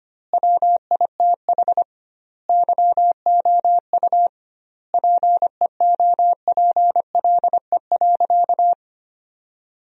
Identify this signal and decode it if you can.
Morse code, heard as a keyed tone, about 25 wpm, 700 Hz. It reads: WITH YOU PEOPLE.
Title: 2x Word-Spacing - Sentences from Top 300 Words - 25wpm